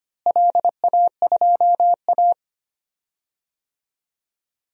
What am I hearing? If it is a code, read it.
Morse code, heard as a keyed tone, about 25 wpm, 700 Hz. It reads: LA2A
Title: DXCC-100 Call Signs - 4 Characters - Repeated 2x - 25wpm